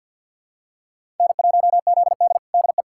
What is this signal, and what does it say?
Morse code, heard as a keyed tone, about 47 wpm, 700 Hz. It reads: N1ZD DE
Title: USA Callsign DE Callsign - 4 Characters - 50wpm